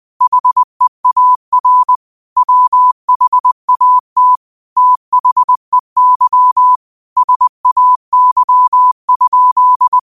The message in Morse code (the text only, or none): HEAR WHAT THEY SAY?